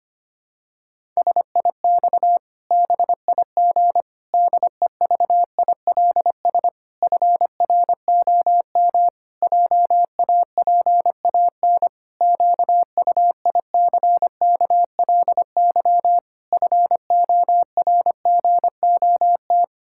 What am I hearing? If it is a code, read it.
Morse code, heard as a keyed tone, about 25 wpm, 700 Hz. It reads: SIX BIG DEVILS FROM JAPAN QUICKLY FORGOT
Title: Pangram Challenge v2 - 25wpm